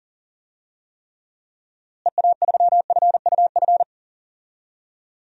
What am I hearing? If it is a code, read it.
Morse code, heard as a keyed tone, about 40 wpm, 700 Hz. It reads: EA3FUF
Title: DXCC-100 Call Signs - 6 Characters - Repeated 3x - 40wpm